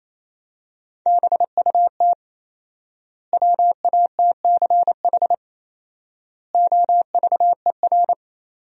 Morse code, heard as a keyed tone, about 28 wpm, 700 Hz. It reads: BUT WATCH OVER
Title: 4x Word-Spacing - Sentences from Top 300 Words - 28wpm